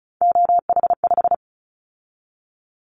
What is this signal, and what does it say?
Morse code, heard as a keyed tone, about 36 wpm, 690 Hz. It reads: OH5
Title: Callsign Prefixes for Top 50 DXCC - 35wpm